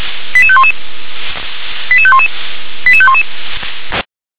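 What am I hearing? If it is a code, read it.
Five-tone selective call (ZVEI), heard as quick runs of five tones, tones of about 70 ms. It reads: 88511 88511 88511
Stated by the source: ZVEI five-tone selective call received on an SDR.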